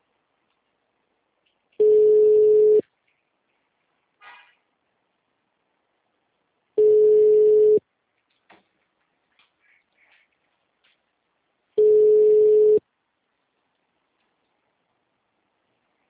Ringback tone (Europe, 425 Hz).
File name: out- -2003- - - .97